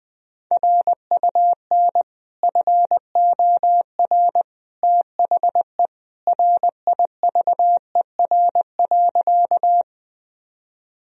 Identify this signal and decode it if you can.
Morse code, heard as a keyed tone, about 20 wpm, 700 Hz. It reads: RUN FOR THE RIVER.
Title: Sentences from Top 1000 Words - Review Part 1 - 20wpm